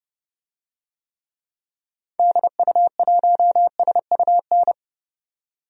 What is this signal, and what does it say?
Morse code, heard as a keyed tone, about 32 wpm, 700 Hz. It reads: DU1SUN
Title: DXCC-100 Call Signs - 6 Characters - 30wpm